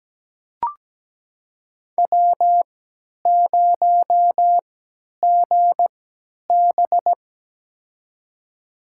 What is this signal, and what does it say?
Morse code, heard as a keyed tone, about 17 wpm, spaced as for 10 wpm, 700 Hz. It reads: W0GB
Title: USA Call Signs - 4 Characters - 3x Character Spacing - 17wpm